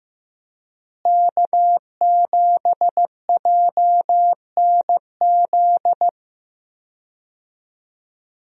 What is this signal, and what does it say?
Morse code, heard as a keyed tone, about 15 wpm, 700 Hz. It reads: K7JNZ
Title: USA Call Signs - 5 Characters - Repeated 3x - 15wpm